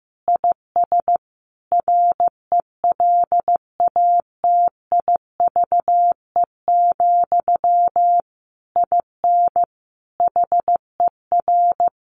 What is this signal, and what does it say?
Morse code, heard as a keyed tone, about 15 wpm, 700 Hz. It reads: IS RELATIVE, IN HER
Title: Sentences from Top 1400 Words 15wpm